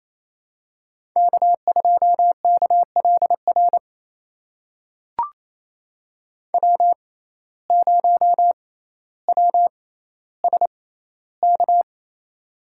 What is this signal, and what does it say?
Morse code, heard as a keyed tone, about 28 wpm, spaced as for 14 wpm, 700 Hz. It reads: K2KLR W0WSK
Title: USA Call Signs - 5 Characters - 6x Character Spacing - 28wpm